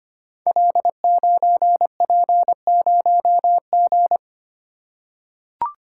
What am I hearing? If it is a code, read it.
Morse code, heard as a keyed tone, about 25 wpm, 700 Hz. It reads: L9P0G
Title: Code Groups - 5 Letter:Numbers - 25wpm